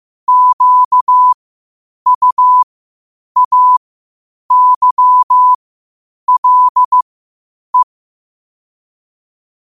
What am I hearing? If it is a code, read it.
Morse code, heard as a keyed tone, about 15 wpm, spaced as for 8 wpm, 1000 Hz. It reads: QUAYLE
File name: qotdfiles / 8wpm